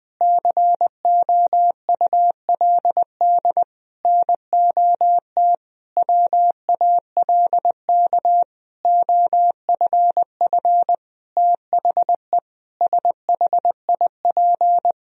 Morse code, heard as a keyed tone, about 21 wpm, 700 Hz. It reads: COULD NOT WALK OFF THE SHIP
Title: Sentences from Top 1000 Words - Review Part 2 - 20wpm